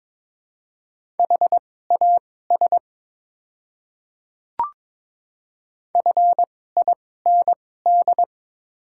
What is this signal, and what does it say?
Morse code, heard as a keyed tone, about 22 wpm, spaced as for 16 wpm, 700 Hz. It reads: HAS FIND
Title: 2x Character Spacing - Top 100 Words - 22wpm